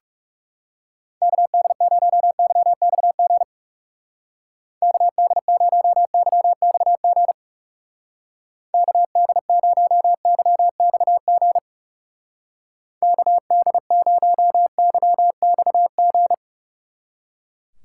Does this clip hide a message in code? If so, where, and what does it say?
Yes: Morse code from 1.2 to 16.3 s, heard as a keyed tone, about 38 wpm, 700 Hz. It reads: KD0YXG KD0YXG KD0YXG KD0YXG